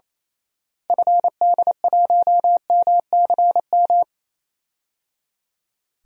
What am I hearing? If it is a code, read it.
Morse code, heard as a keyed tone, about 28 wpm, 700 Hz. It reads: FD1MCM
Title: DXCC-100 Call Signs - 6 Characters - Repeated 3x - 28wpm